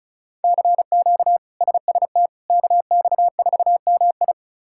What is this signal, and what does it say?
Morse code, heard as a keyed tone, about 36 wpm, 700 Hz. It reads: CQ SST KX4MI
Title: K1USN SST QSO Practice - 35wpm